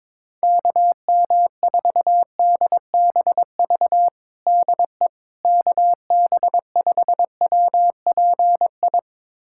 Morse code, heard as a keyed tone, about 22 wpm, 700 Hz. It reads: KM4DBV DE KB5WPI
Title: USA Callsign DE Callsign - 6 Characters - 22wpm